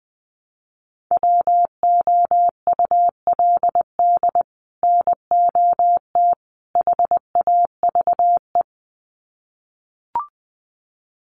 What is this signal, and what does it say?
Morse code, heard as a keyed tone, about 20 wpm, 700 Hz. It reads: WOULD NOT HAVE
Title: Sets of 3 Words 20wpm